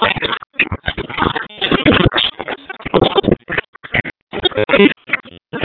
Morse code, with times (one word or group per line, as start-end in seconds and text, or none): none